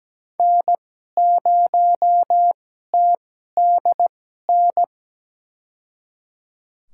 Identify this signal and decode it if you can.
Morse code, heard as a keyed tone, about 17 wpm, spaced as for 12 wpm, 700 Hz. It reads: N0TDN
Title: USA Call Signs - 2x Character Spacing - 17wpm